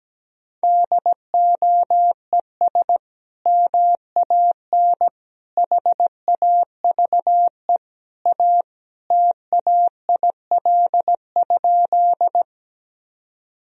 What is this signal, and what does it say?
Morse code, heard as a keyed tone, about 17 wpm, 700 Hz. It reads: DOES MAN HAVE A TAIL?
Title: Sentences from Top 1000 Words - Review Part 3 - 17wpm